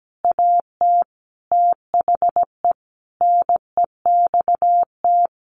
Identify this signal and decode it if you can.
Morse code, heard as a keyed tone, about 17 wpm, 700 Hz. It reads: AT THE NEXT
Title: Sentences from Top 1500 Words 17wpm